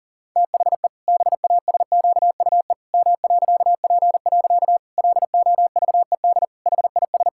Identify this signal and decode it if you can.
Morse code, heard as a keyed tone, about 37 wpm, 700 Hz. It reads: THE BASQUE M.P. LOVED HIS
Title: Pangram Challenge v2 - 40wpm